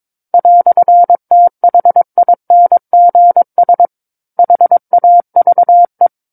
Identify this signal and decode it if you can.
Morse code, heard as a keyed tone, about 22 wpm, 700 Hz. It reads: "THINGS HAVE
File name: Book One / 13-Chapter 13 - 22 WPM